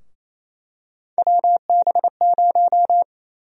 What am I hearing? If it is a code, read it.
Morse code, heard as a keyed tone, about 28 wpm, 700 Hz. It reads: WB0
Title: USA Call Sign Prefixes - Repeated 2x - 28wpm